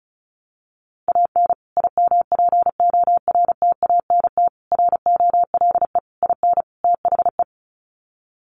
Morse code, heard as a keyed tone, about 34 wpm, 700 Hz. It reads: AN IMPORTANT ROLE IN THE
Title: Sets of 5 Words Encore2 35wpm